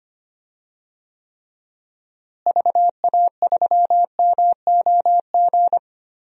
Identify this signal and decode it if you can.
Morse code, heard as a keyed tone, about 25 wpm, 700 Hz. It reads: VA3MOG
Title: DXCC-100 Call Signs - 6 Characters - Repeated 2x - 25wpm